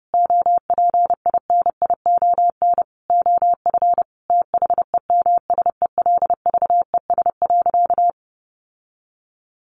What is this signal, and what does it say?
Morse code, heard as a keyed tone, about 30 wpm, 700 Hz. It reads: OPINION OF THEMSELVES.
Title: Sentences from Top 1300 Words 30wpm